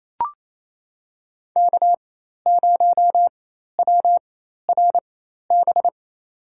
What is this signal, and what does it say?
Morse code, heard as a keyed tone, about 28 wpm, spaced as for 13 wpm, 700 Hz. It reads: K0WRB
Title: USA Call Signs - 5 Characters - 4x Character Spacing - 28wpm